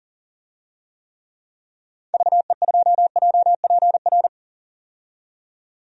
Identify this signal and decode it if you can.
Morse code, heard as a keyed tone, about 40 wpm, 700 Hz. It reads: VE2JPR
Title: DXCC-100 Call Signs - 6 Characters - 40wpm